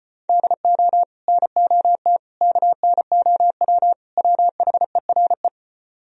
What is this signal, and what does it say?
Morse code, heard as a keyed tone, about 33 wpm, 700 Hz. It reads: DO NOT KNOW WHERE